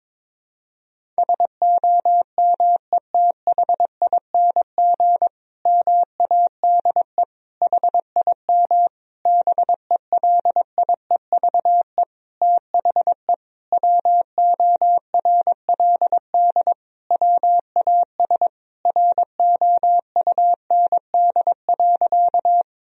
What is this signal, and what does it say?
Morse code, heard as a keyed tone, about 22 wpm, 700 Hz. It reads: SOMETHING MADE HIM BELIEVE THE WORLD WAS ROUND.
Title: Sentences from Top 1000 Words - Review Part 1 - 22wpm